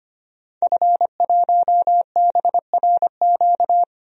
Morse code, heard as a keyed tone, about 25 wpm, 700 Hz. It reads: F1BRQ
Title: DXCC-100 Call Signs - 5 Characters - Repeated 3x - 25wpm